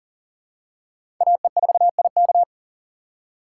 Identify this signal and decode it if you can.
Morse code, heard as a keyed tone, about 40 wpm, 700 Hz. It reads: AE4IK
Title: USA Call Signs - 6x Character Spacing - 40wpm